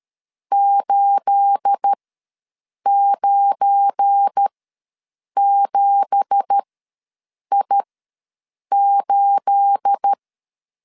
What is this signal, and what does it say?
Morse code, heard as a keyed tone, about 13 wpm, spaced as for 7 wpm, 800 Hz. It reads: 897I8
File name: LEZIONI / 9 2867 ws 12wpm 3431